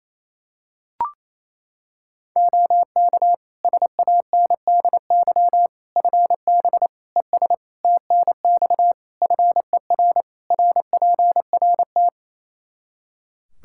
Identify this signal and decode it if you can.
Morse code, heard as a keyed tone, about 28 wpm, 700 Hz. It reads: OK SANDY FB ES TNX FER RPRT
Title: K1USN SST QSO Practice 28wpm